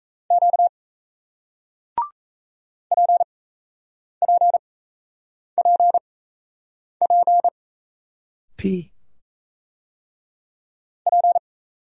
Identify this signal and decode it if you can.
Morse code, heard as a keyed tone, about 36 wpm, spaced as for 8 wpm, 700 Hz. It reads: Q PPPP P